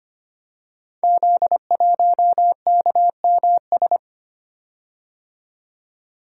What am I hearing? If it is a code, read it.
Morse code, heard as a keyed tone, about 25 wpm, 700 Hz. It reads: Z1KMS
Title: Code Groups - 5 Letter:Numbers - 25wpm